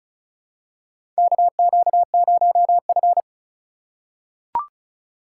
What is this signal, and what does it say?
Morse code, heard as a keyed tone, about 34 wpm, 700 Hz. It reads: KQ0F
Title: USA Call Signs - 4 Characters - 4x Character Spacing - 35wpm